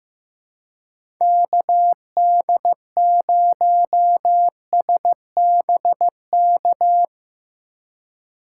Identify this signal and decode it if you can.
Morse code, heard as a keyed tone, about 15 wpm, 700 Hz. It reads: KD0SBK
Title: USA Call Signs - 2x Character Spacing - 15wpm